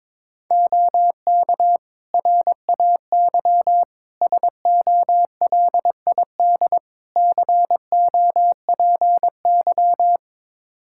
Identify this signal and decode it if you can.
Morse code, heard as a keyed tone, about 22 wpm, 700 Hz. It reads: OK RAY SOLID COPY
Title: Practice QSO - Exchange 3 - More Info - 22wpm